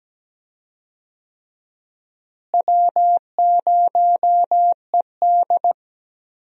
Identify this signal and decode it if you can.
Morse code, heard as a keyed tone, about 17 wpm, 700 Hz. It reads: W0ED
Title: USA Call Signs - 4 Characters - Repeated 3x - 17wpm